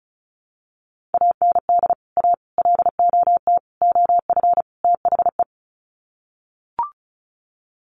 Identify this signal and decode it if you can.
Morse code, heard as a keyed tone, about 34 wpm, 700 Hz. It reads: AND A LOT OF THE
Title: Sets of 5 Words Encore 35wpm